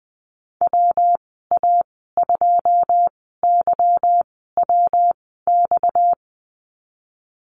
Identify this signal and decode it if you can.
Morse code, heard as a keyed tone, about 20 wpm, spaced as for 14 wpm, 700 Hz. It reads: WA2YWX
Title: 245 Lesson 39g - Letter z - Call Signs 20wpm